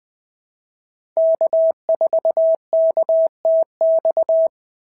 Morse code, heard as a keyed tone, about 20 wpm, 690 Hz. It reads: K4KTX
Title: SOTA QSO Practice 20wpm